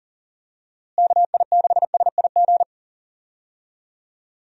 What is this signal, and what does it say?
Morse code, heard as a keyed tone, about 40 wpm, 700 Hz. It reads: KI6SIG
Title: USA Call Signs - 6 Characters - Repeated 3x - 40wpm